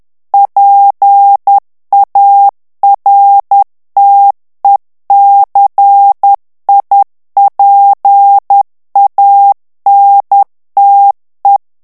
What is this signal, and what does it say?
Morse code, heard as a keyed tone, about 10 wpm, 800 Hz. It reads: PARTECIPANTE
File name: ITALIEN texte clair 2 vitesse 10 mots minute